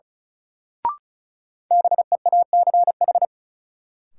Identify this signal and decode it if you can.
Morse code, heard as a keyed tone, about 34 wpm, 700 Hz. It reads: BEACH
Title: Sentences from Top 1500 Words - New Word Drill Alt - 35wpm